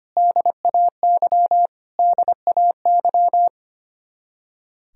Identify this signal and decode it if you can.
Morse code, heard as a keyed tone, about 25 wpm, 700 Hz. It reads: DAY DAY
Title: Top 100 Words Repeated 25wpm